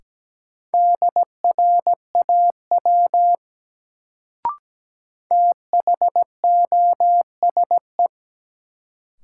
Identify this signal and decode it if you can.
Morse code, heard as a keyed tone, about 17 wpm, 700 Hz. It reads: DRAW THOSE
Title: Sentences from Top 300 Words - New Word Drill - 17wpm